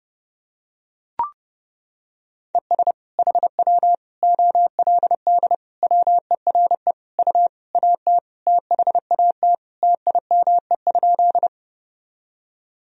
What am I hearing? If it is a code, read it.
Morse code, heard as a keyed tone, about 30 wpm, 700 Hz. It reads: ES HW OLD WERE U AT THAT TIME?
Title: Sentences from Top 200 Words and Abbreviations 30wpm